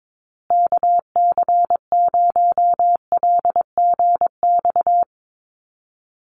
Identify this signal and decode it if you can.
Morse code, heard as a keyed tone, about 22 wpm, 700 Hz. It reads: KC0LGX
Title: USA Call Signs Repeated 2x 22wpm